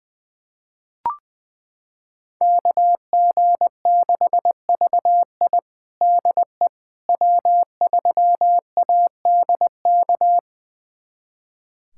Morse code, heard as a keyed tone, about 20 wpm, 700 Hz. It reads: KG6VI DE W3ADK
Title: USA Callsign DE Callsign - 5 Characters - 20wpm